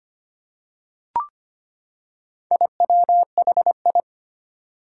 Morse code, heard as a keyed tone, about 25 wpm, 700 Hz. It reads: IWHI
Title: Four Challenging Letter-Number Combinations 25wpm